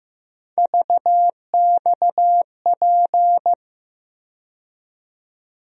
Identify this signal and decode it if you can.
Morse code, heard as a keyed tone, about 15 wpm, 700 Hz. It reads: VXP
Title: USA Call Sign Postfixes - 3 Letters 15wpm